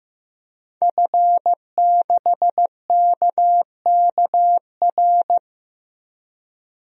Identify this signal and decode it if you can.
Morse code, heard as a keyed tone, about 15 wpm, 700 Hz. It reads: F6KKR
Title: DXCC-100 Call Signs - 5 Characters - 15wpm